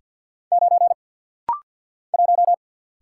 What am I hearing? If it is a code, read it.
Morse code, heard as a keyed tone, about 48 wpm, 700 Hz. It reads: 9 1
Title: Single Letter-Number Alt - Rapid-Fire 50wpm